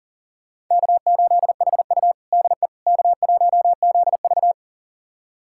Morse code, heard as a keyed tone, about 37 wpm, 700 Hz. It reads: K8HU DE K1ZV
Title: USA Callsign DE Callsign - 4 Characters - 40wpm